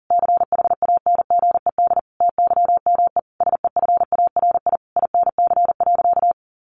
Morse code, heard as a keyed tone, about 40 wpm, 700 Hz. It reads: CHANGED TYME SEFARI INC.
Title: A Sound Of Thunder 40wpm